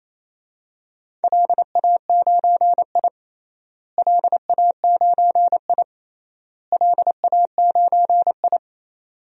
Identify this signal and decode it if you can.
Morse code, heard as a keyed tone, about 27 wpm, 700 Hz. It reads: LA9I LA9I LA9I
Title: DXCC-100 Call Signs - 4 Characters - Repeated 3x - 28wpm